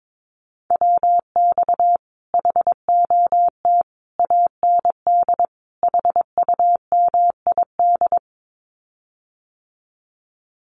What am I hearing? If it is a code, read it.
Morse code, heard as a keyed tone, about 22 wpm, 700 Hz. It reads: WX HOT AND HUMID
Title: Common QSO Phrases Encore 22wpm